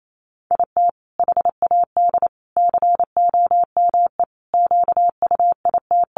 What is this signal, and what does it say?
Morse code, heard as a keyed tone, about 28 wpm, 700 Hz. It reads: IT HAD COME QUIT
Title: Sentences from Top 1300 Words 28wpm